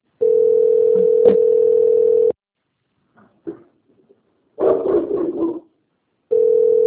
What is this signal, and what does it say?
Ringback tone (North America), 440 + 480 Hz.